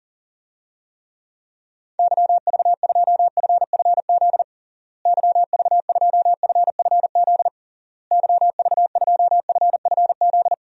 Morse code, heard as a keyed tone, about 37 wpm, 700 Hz. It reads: YV2FFZ YV2FFZ YV2FFZ
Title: DXCC-100 Call Signs - 6 Characters - Repeated 3x - 40wpm